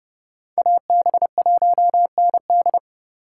Morse code, heard as a keyed tone, about 31 wpm, 700 Hz. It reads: AB1ND